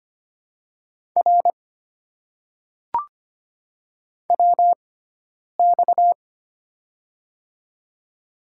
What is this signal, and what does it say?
Morse code, heard as a keyed tone, about 25 wpm, spaced as for 9 wpm, 700 Hz. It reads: R WX
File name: Top 90 QSO Elements - 6x Character Spacing - 25wpm